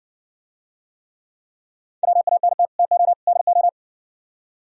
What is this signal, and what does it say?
Morse code, heard as a keyed tone, about 58 wpm, 700 Hz. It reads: WANT TO DO